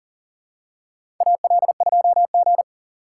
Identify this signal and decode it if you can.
Morse code, heard as a keyed tone, about 40 wpm, 700 Hz. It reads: AL2G
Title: USA Call Signs - 4 Characters - 2x Character Spacing - 40wpm